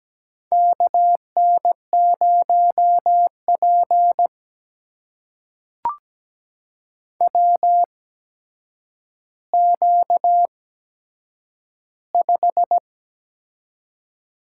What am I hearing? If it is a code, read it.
Morse code, heard as a keyed tone, about 17 wpm, spaced as for 8 wpm, 700 Hz. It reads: KN0P WQ5
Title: USA Call Signs - 4 Characters - 8x Character Spacing - 17wpm